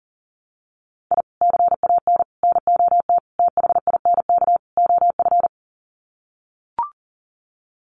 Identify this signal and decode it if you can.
Morse code, heard as a keyed tone, about 40 wpm, 700 Hz. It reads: I CAN NOT THINK OF
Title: Sets of 5 Words 40wpm